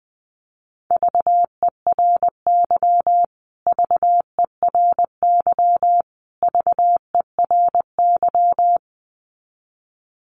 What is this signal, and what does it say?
Morse code, heard as a keyed tone, about 20 wpm, 700 Hz. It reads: VERY VERY VERY
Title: Top 500 Words Repeated 20wpm